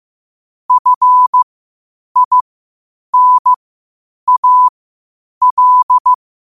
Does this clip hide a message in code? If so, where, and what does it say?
Yes: Morse code from 0.7 to 6.1 s, heard as a keyed tone, about 15 wpm, spaced as for 8 wpm, 1000 Hz. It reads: FINAL